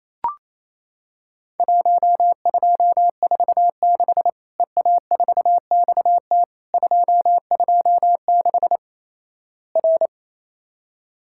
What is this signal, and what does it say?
Morse code, heard as a keyed tone, about 28 wpm, 700 Hz. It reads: 1246 EA4XT 226 R
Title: Worked All Europe DX Contest - QTC Exchange - 28wpm